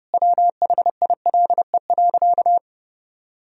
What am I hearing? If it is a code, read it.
Morse code, heard as a keyed tone, about 30 wpm, 700 Hz. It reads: WHILE.
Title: Sentences from Top 500 Words and Abbreviations 30wpm